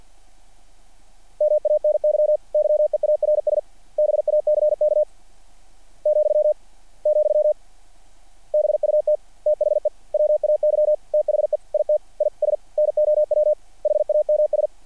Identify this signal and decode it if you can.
Morse code, heard as a keyed tone, about 48 wpm, spaced as for 38 wpm, 600 Hz. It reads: MANY YEARS BACK,, BUT THE WAY THE IT IS NOW HAMS
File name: archived / CRSnet